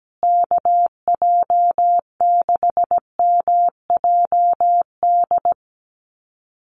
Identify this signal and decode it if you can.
Morse code, heard as a keyed tone, about 17 wpm, 700 Hz. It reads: KJ6MJD
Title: USA Call Signs Repeated 3x 17wpm